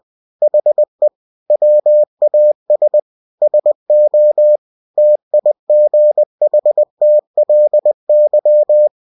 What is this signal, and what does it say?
Morse code, heard as a keyed tone, about 20 wpm, 600 Hz. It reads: HE WAS SO TIGHTLY
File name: cw-books / winnie-the-pooh / winnie-20wpm-page-0019